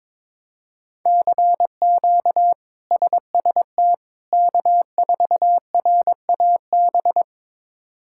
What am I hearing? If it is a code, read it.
Morse code, heard as a keyed tone, about 22 wpm, 700 Hz. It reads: CQ SST K4RAB
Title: K1USN SST QSO Practice - 22wpm